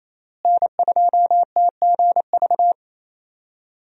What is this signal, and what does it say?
Morse code, heard as a keyed tone, about 28 wpm, 700 Hz. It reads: N2TGV
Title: USA Call Signs - 5 Characters - Repeated 3x - 28wpm